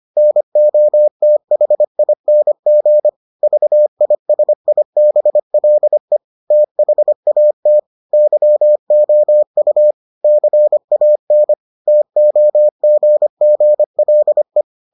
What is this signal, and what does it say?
Morse code, heard as a keyed tone, about 25 wpm, 600 Hz. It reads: NOTHING VISIBLE THAT YOU CAN TOGGLE